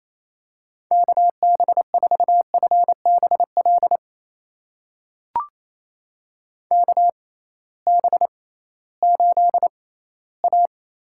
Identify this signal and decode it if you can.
Morse code, heard as a keyed tone, about 28 wpm, spaced as for 16 wpm, 700 Hz. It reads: KB4FBL KB8A
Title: USA Call Signs - 6 Characters - 6x Character Spacing - 28wpm